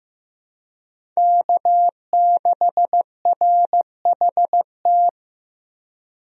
Morse code, heard as a keyed tone, about 15 wpm, 700 Hz. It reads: K6RHT
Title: USA Call Signs - 5 Characters - 15wpm